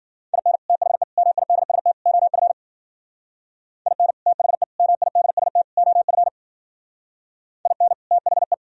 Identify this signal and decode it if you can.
Morse code, heard as a keyed tone, about 52 wpm, 700 Hz. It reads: IN THE MIDST OF IN THE MIDST OF IN THE